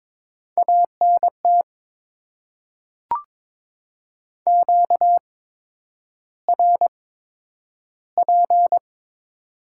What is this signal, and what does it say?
Morse code, heard as a keyed tone, about 21 wpm, spaced as for 9 wpm, 700 Hz. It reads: ANT QRP
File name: Top 90 QSO Elements - 8x Character Spacing - 22wpm